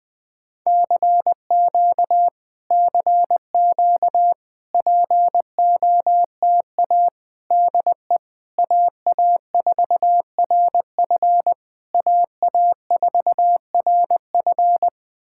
Morse code, heard as a keyed tone, about 20 wpm, 700 Hz. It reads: CQ CQ POTA DE AA4RF AA4RF
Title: POTA QSO Practice - 20wpm